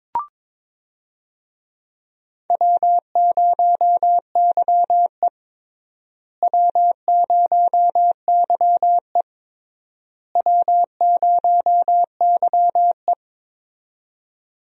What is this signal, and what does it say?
Morse code, heard as a keyed tone, about 22 wpm, 700 Hz. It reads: W0YE W0YE W0YE
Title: USA Call Signs - 4 Characters - Repeated 3x - 22wpm